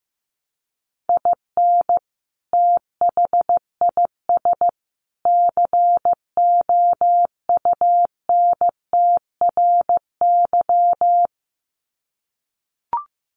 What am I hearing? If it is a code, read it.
Morse code, heard as a keyed tone, about 15 wpm, 700 Hz. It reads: IN THIS COUNTRY
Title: Sets of 3 Words 15wpm